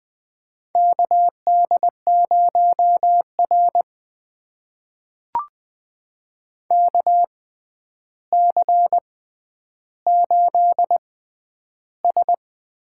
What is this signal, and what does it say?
Morse code, heard as a keyed tone, about 20 wpm, spaced as for 10 wpm, 700 Hz. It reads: KD0R KC8S
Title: USA Call Signs - 6x Character Spacing - 20wpm